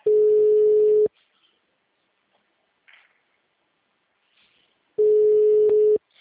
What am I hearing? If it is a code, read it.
Ringback tone (Europe, 425 Hz).
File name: out- -2019- - - .46953